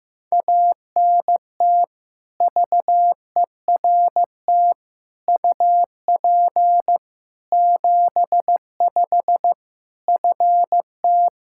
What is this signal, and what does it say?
Morse code, heard as a keyed tone, about 16 wpm, 700 Hz. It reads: ANT VERT UP 75 FT
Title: Practice QSO - Exchange 2 - Info - 15wpm